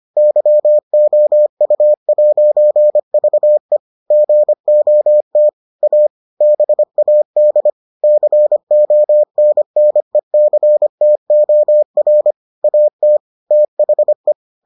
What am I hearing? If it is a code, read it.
Morse code, heard as a keyed tone, about 25 wpm, 600 Hz. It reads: YOU'VE GOT A BAD CONNECTOR AT THE